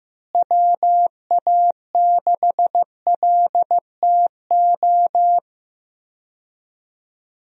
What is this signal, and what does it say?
Morse code, heard as a keyed tone, about 15 wpm, 700 Hz. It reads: WA6LTO
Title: USA Call Signs - 6 Characters - Repeated 2x - 15wpm